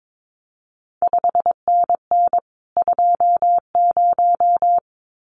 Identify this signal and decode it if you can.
Morse code, heard as a keyed tone, about 22 wpm, 700 Hz. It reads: 5NN 20
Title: CQ WW DX Contest Exchange Encore - 22wpm